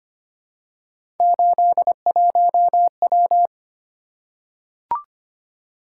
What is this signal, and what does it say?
Morse code, heard as a keyed tone, about 26 wpm, 700 Hz. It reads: 81W
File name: Three Letter-Number Combinations - 25wpm